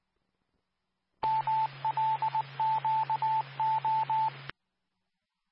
Morse code, heard as a keyed tone, about 19 wpm, 880 Hz, with radio noise. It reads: MLQO
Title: Nassau and Suffolk County Amateur Repeaters